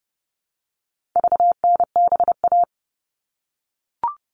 Morse code, heard as a keyed tone, about 28 wpm, 700 Hz. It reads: VNBA
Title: All Cut Numbers - 4-Digit 30wpm